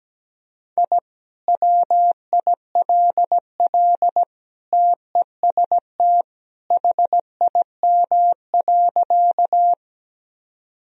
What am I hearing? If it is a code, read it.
Morse code, heard as a keyed tone, about 17 wpm, 700 Hz. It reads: I WILL TEST HIM.
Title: Sentences from Top 1000 Words - Review Part 3 - 17wpm